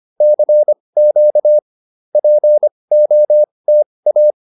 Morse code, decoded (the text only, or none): CQ POTA